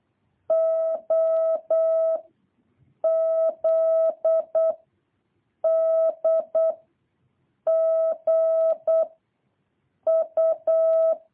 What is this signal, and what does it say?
Morse code, heard as a keyed tone, about 8 wpm, 650 Hz. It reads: O Z D G U
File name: jh8cbh / 2023cw